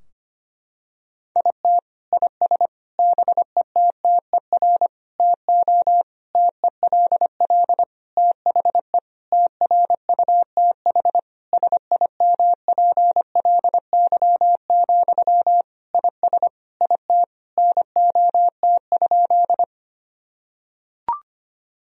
Morse code, heard as a keyed tone, about 25 wpm, 700 Hz. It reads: IT IS BETTER TO TELL THE TRUTH SIMPLY, IS IT NOT?
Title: Sentences from Top 1000 Words - Review Part 4 - 25wpm